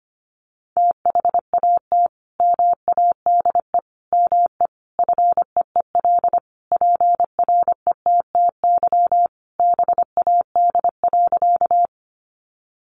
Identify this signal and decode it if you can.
Morse code, heard as a keyed tone, about 25 wpm, 700 Hz. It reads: THAT MADE ME FEEL PRETTY BAD.
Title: Sentences from Top 1000 Words 25wpm